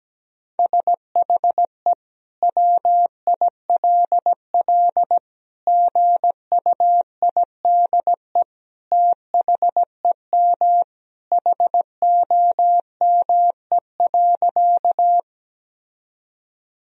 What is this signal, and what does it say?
Morse code, heard as a keyed tone, about 17 wpm, 700 Hz. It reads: SHE WILL GUIDE THEM HOME.
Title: Sentences from Top 1000 Words - Review Part 3 - 17wpm